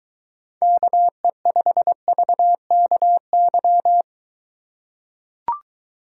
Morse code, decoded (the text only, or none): KE5VKY